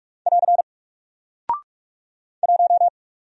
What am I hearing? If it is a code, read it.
Morse code, heard as a keyed tone, about 45 wpm, 700 Hz. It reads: <AR> 1
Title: Single Character (Letter:Number:Punctuation:Prosign) Alt 45wpm